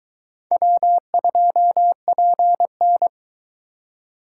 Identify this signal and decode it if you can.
Morse code, heard as a keyed tone, about 22 wpm, 700 Hz. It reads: W2PN